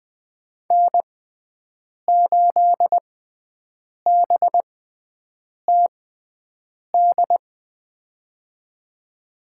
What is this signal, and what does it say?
Morse code, heard as a keyed tone, about 20 wpm, spaced as for 7 wpm, 700 Hz. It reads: N8BTD
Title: USA Call Signs - 5 Characters - 6x Character Spacing - 20wpm